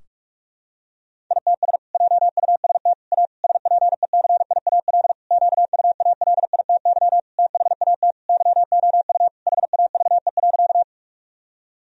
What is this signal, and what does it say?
Morse code, heard as a keyed tone, about 41 wpm, 700 Hz. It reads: ITS JUST A SPECIAL QUALITY THAT YOU HAVE.
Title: Sentences from Top 1000 Words - Review Part 4 - 45wpm